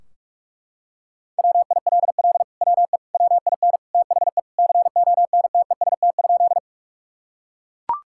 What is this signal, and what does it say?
Morse code, heard as a keyed tone, about 41 wpm, 700 Hz. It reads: WILL WE WIN THE CONTEST?
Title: Sentences from Top 1600 Words 45wpm